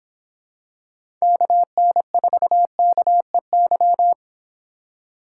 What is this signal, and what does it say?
Morse code, heard as a keyed tone, about 25 wpm, 700 Hz. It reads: KN4KEY